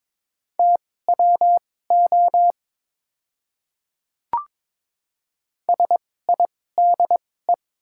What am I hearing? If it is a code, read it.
Morse code, heard as a keyed tone, about 21 wpm, spaced as for 16 wpm, 700 Hz. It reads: TWO SIDE
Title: 2x Character Spacing - Top 100 Words - 22wpm